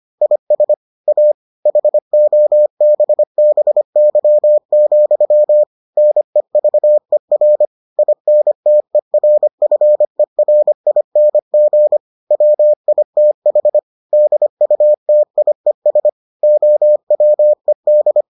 Morse code, heard as a keyed tone, about 25 wpm, 600 Hz. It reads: IS A HOBBY, NEVER INTERFERING WITH DUTIES OWED